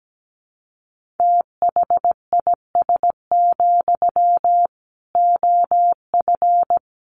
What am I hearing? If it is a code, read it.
Morse code, heard as a keyed tone, about 17 wpm, 700 Hz. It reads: THIS, OF
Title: Sentences from Top 1400 Words 17wpm